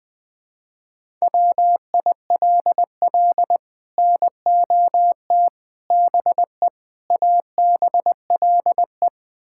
Morse code, heard as a keyed tone, about 20 wpm, 700 Hz. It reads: WILL NOT BE ABLE
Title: Build A Phrase - 5-Word Phrases - 20wpm